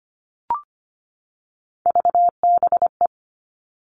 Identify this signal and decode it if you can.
Morse code, heard as a keyed tone, about 25 wpm, 700 Hz. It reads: VBE
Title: All Cut Numbers - 3-Digit 25wpm